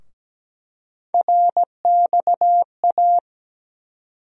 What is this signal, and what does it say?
Morse code, heard as a keyed tone, about 17 wpm, 700 Hz. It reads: RXA
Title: USA Call Sign Postfixes - 3 Letters 17wpm